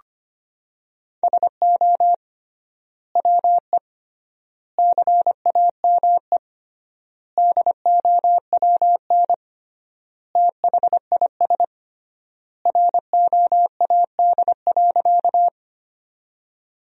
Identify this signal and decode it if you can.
Morse code, heard as a keyed tone, about 25 wpm, 700 Hz. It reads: SO WE CAME DOWN THIS ROAD.
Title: 3x Word-Spacing - Sentences from Top 400 Words - 25wpm